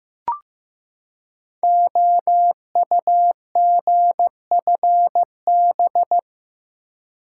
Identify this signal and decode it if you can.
Morse code, heard as a keyed tone, about 15 wpm, 700 Hz. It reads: OUGFB
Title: Code Groups - 5 Letters - 15wpm